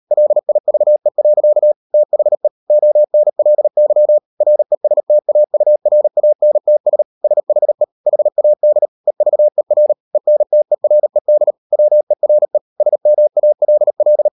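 Morse code, heard as a keyed tone, about 38 wpm, 600 Hz. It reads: LIVE. THE ONLY RESTAURANTS SHE HAD EVER ENTERED WERE SMALL
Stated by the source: Morse code practice at 38 WPM, sending Ken Follett's novel Never in English, from CW med Gnister.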